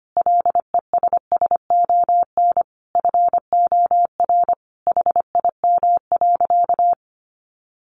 Morse code, heard as a keyed tone, about 25 wpm, 700 Hz. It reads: LESSON FOR HIM.
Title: Sentences from Top 1200 Words 25wpm